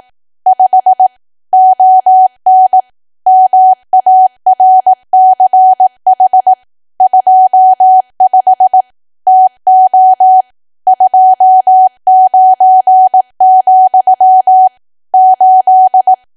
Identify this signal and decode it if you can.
Morse code, heard as a keyed tone, about 18 wpm, 750 Hz. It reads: 5 ON MARCH 25 TO 29, 8
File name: cwa / W1AW-Level2 / 18WPM